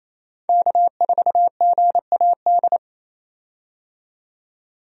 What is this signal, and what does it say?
Morse code, heard as a keyed tone, about 28 wpm, 700 Hz. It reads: K4GAD
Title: USA Call Signs - 5 Characters - 28wpm